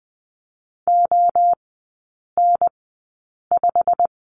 Morse code, heard as a keyed tone, about 20 wpm, spaced as for 8 wpm, 690 Hz. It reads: ON5
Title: 087 Lesson 15e - Number 5 - Three Characters 20wpm